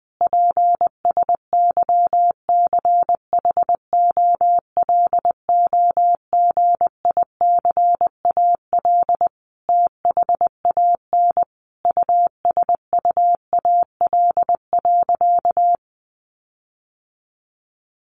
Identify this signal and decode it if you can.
Morse code, heard as a keyed tone, about 20 wpm, 700 Hz. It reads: PSYCHOLOGICAL THAN USUAL.
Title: Sentences from Top 1400 Words 20wpm